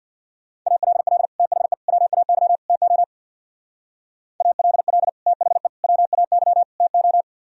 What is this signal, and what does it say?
Morse code, heard as a keyed tone, about 64 wpm, 700 Hz. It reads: ALL THE WAY TO ALL THE WAY TO